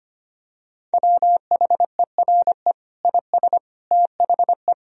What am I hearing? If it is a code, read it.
Morse code, heard as a keyed tone, about 25 wpm, 700 Hz. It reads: WHERE IS THE
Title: Sentences from Top 1000 Words - Review Part 4 - 25wpm